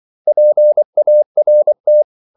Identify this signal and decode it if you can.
Morse code, heard as a keyed tone, about 24 wpm, 600 Hz. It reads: PART
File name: CW / Practice / 500MostCommonEnglishWords24